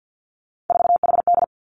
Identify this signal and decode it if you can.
Morse code, heard as a keyed tone, about 50 wpm, 700 Hz. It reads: 4HD
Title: Three Challenging Letter-Number Combinations 50wpm